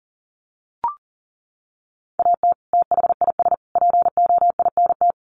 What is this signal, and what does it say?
Morse code, heard as a keyed tone, about 40 wpm, 700 Hz. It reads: AT THIS POINT
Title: Sets of 3 Words 40wpm